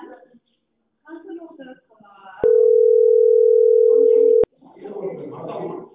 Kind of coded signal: telephone call progress tone